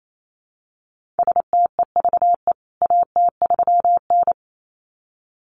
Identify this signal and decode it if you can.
Morse code, heard as a keyed tone, about 28 wpm, 700 Hz. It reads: STEVE AT3N
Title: CWT Exchange 28wpm